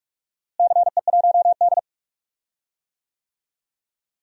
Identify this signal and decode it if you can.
Morse code, heard as a keyed tone, about 46 wpm, 700 Hz. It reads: CE1D
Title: DXCC-100 Call Signs - 4 Characters - Repeated 3x - 45wpm